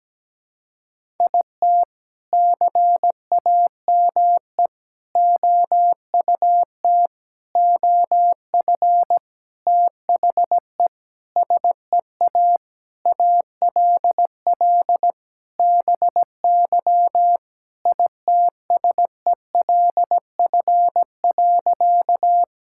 Morse code, heard as a keyed tone, about 17 wpm, 700 Hz. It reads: IT CAME OUT OF THE SEA ALL BY ITSELF.
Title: Sentences from Top 1000 Words - Review Part 5 - 17wpm